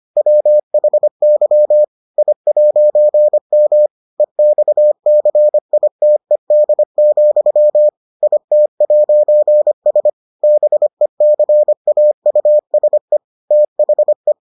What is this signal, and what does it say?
Morse code, heard as a keyed tone, about 25 wpm, 600 Hz. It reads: WHY I'M EXCITED, IT'S BECAUSE THE